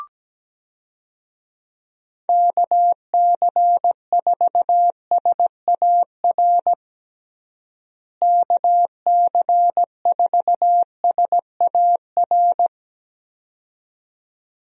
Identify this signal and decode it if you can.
Morse code, heard as a keyed tone, about 17 wpm, 700 Hz. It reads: KC4SAR KC4SAR
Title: USA Call Signs - 6 Characters - Repeated 2x - 17wpm